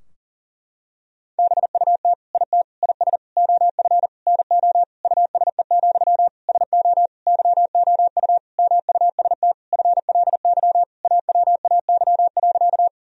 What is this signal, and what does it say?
Morse code, heard as a keyed tone, about 40 wpm, 700 Hz. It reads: BUT IT IS OF NO USE, SO YOU MUST FLY AWAY.
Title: Sentences from Top 1000 Words - Review Part 2 - 40wpm